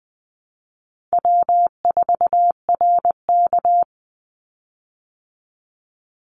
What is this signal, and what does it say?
Morse code, heard as a keyed tone, about 20 wpm, 700 Hz. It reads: W4RK
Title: Easy Call Signs Repeated 2x - 20wpm